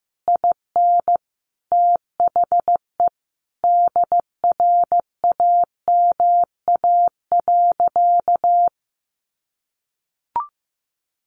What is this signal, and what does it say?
Morse code, heard as a keyed tone, about 15 wpm, 700 Hz. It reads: IN THE DRAMA.
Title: Sentences from Top 1500 Words 15wpm